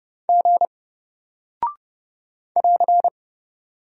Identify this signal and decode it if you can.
Morse code, heard as a keyed tone, about 30 wpm, 700 Hz. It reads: G <AR>
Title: Single Character (Letter:Number:Punctuation:Prosign) Alt 30wpm